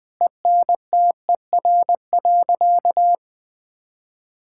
Morse code, heard as a keyed tone, about 20 wpm, 700 Hz. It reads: ENTER.
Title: Sentences with 6-Word Phrases - Spoken Beginning - 20wpm